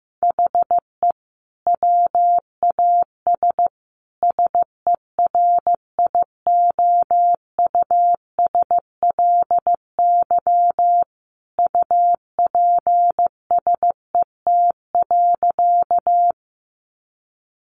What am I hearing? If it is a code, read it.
Morse code, heard as a keyed tone, about 15 wpm, 700 Hz. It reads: HE WAS SERIOUSLY UPSET.
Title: Sentences from Top 1200 Words 15wpm